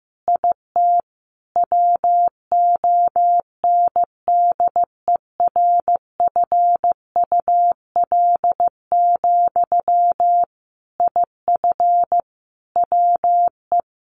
Morse code, heard as a keyed tone, about 14 wpm, 700 Hz. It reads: IT WONDERFUL, IF WE
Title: Sentences from Top 1100 Words 15wpm